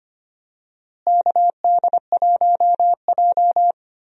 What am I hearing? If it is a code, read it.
Morse code, heard as a keyed tone, about 25 wpm, 700 Hz. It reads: KD1J
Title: USA Call Signs - 4 Characters - Repeated 2x - 25wpm